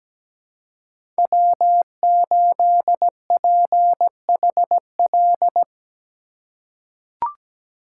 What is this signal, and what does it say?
Morse code, heard as a keyed tone, about 17 wpm, 700 Hz. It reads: W8PHL
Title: USA Call Signs - 5 Characters - 2x Character Spacing - 17wpm